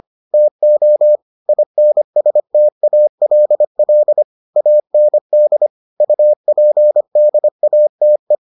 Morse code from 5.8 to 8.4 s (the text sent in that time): UPDATE